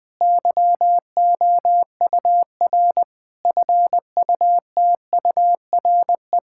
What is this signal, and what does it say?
Morse code, heard as a keyed tone, about 20 wpm, 700 Hz. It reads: YOUR FUTURE
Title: Difficult Sentences with Hints - 20wpm